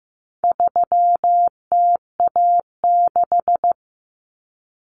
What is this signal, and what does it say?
Morse code, heard as a keyed tone, about 15 wpm, 700 Hz. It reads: 3TA6
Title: Common Cut Numbers - 4-Digit 15wpm